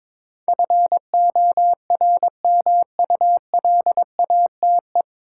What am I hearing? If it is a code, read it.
Morse code, heard as a keyed tone, about 22 wpm, 700 Hz. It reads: FORMULATE
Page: Morse Code Ninja practice 22wpm